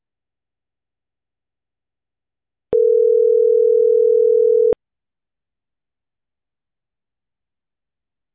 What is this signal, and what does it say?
Ringback tone (North America), 440 + 480 Hz.